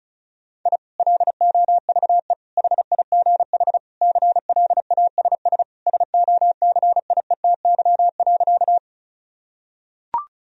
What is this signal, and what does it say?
Morse code, heard as a keyed tone, about 36 wpm, 700 Hz. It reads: I LOVE HIGH CLASS SOCIETY.
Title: Sentences from Top 1000 Words - Review Part 4 - 35wpm